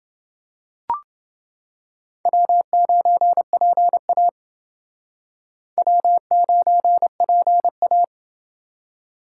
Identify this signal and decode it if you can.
Morse code, heard as a keyed tone, about 28 wpm, 700 Hz. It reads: W9PA W9PA